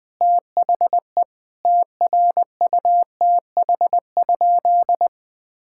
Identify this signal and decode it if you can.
Morse code, heard as a keyed tone, about 20 wpm, 700 Hz. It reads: THE TRUTH?
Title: Sentences from Top 1000 Words - Review Part 6 - 20wpm